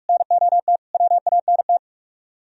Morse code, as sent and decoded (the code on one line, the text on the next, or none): -. --- - / .-- .- -. -
NOT WANT